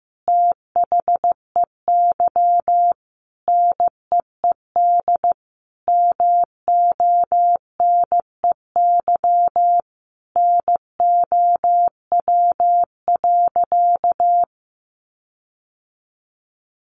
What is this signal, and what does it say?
Morse code, heard as a keyed tone, about 15 wpm, 700 Hz. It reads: THEY NEED MONEY NOW.
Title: Sentences from Top 1300 Words 15wpm